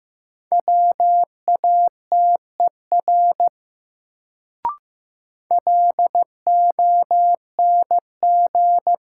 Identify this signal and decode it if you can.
Morse code, heard as a keyed tone, about 15 wpm, 700 Hz. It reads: WATER LONG
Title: Sentences from Top 100 Words - New Word Drill - 15wpm